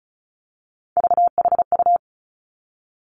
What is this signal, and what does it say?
Morse code, heard as a keyed tone, about 36 wpm, 700 Hz. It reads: VHU
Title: Three Challenging Letter-Number Combinations 35wpm